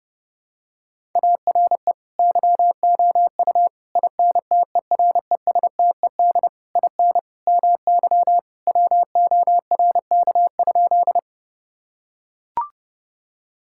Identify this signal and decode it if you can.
Morse code, heard as a keyed tone, about 32 wpm, 700 Hz. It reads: ARE YOU INTERESTED IN MY WORK?
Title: Sentences from Top 1000 Words - Review Part 6 - 30wpm